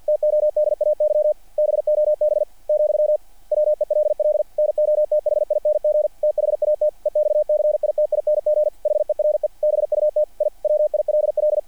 Morse code, heard as a keyed tone, about 50 wpm, 600 Hz. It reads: TODAY BOB , WELL NOTHING THAT EXCITING HERE BUT I WILL
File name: archived / CRSnet